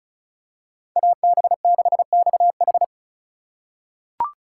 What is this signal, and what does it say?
Morse code, heard as a keyed tone, about 34 wpm, 700 Hz. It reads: AB6XH
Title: USA Call Signs - 5 Characters - 6x Character Spacing - 35wpm